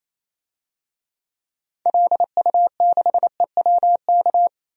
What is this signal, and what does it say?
Morse code, heard as a keyed tone, about 28 wpm, 700 Hz. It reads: LU6EWK
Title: DXCC-100 Call Signs - 6 Characters - 28wpm